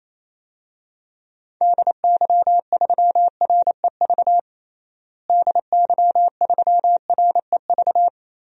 Morse code, heard as a keyed tone, about 28 wpm, 700 Hz. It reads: DY3REV DY3REV
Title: DXCC-100 Call Signs - 6 Characters - Repeated 3x - 28wpm